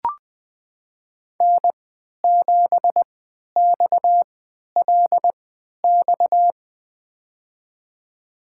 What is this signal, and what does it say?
Morse code, heard as a keyed tone, about 20 wpm, spaced as for 11 wpm, 700 Hz. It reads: N7XLX
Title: USA Call Signs - 5 Characters - 3x Character Spacing - 20wpm